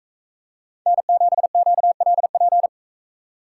Morse code, heard as a keyed tone, about 42 wpm, 700 Hz. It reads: N7QLP